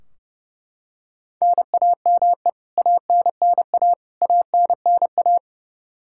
Morse code, heard as a keyed tone, about 30 wpm, 700 Hz. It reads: NAME ANNA ANNA
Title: K1USN SST QSO Practice 30wpm